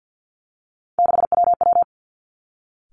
Morse code, heard as a keyed tone, about 50 wpm, 700 Hz. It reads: 6WW
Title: Three Challenging Letter-Number Combinations 50wpm